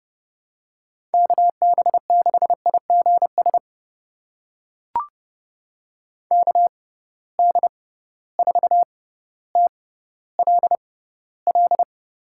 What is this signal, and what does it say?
Morse code, heard as a keyed tone, about 30 wpm, spaced as for 15 wpm, 700 Hz. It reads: KB6IGS KD4TLL
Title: USA Call Signs - 6 Characters - 6x Character Spacing - 30wpm